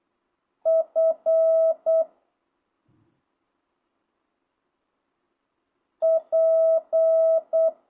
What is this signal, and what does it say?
Morse code, heard as a keyed tone, about 8 wpm, 650 Hz. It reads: F P
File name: jh8cbh / 2025cw